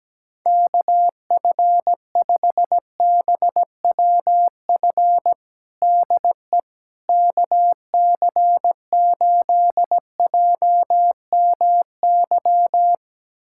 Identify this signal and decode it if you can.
Morse code, heard as a keyed tone, about 17 wpm, 700 Hz. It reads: KF5BWF DE KC8JMY
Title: USA Callsign DE Callsign - 6 Characters - 17wpm